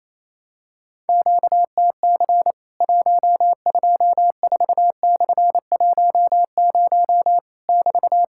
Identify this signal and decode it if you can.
Morse code, heard as a keyed tone, about 27 wpm, 700 Hz. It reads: QTC 124/10 <BT>
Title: Worked All Europe DX Contest - QTC Exchange - 28wpm